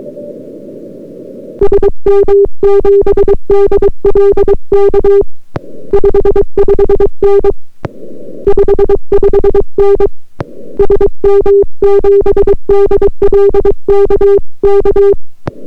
Morse code, heard as a keyed tone, about 22 wpm, 400 Hz. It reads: SM7DLK 55N 55N SM7DLK K